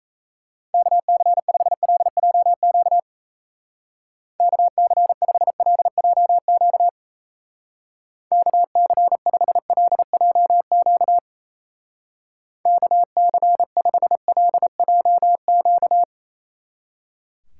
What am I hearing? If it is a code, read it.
Morse code, heard as a keyed tone, about 35 wpm, 700 Hz. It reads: KC5LJQ KC5LJQ KC5LJQ KC5LJQ
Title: Speed Racing - USA 2x3 Call Signs 28wpm